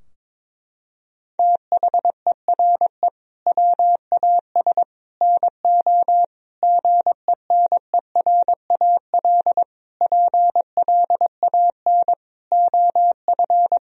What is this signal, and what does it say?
Morse code, heard as a keyed tone, about 22 wpm, 700 Hz. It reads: THERE WAS NO GENERAL PLAN OF
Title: Sentences from Top 1000 Words - Review Part 4 - 22wpm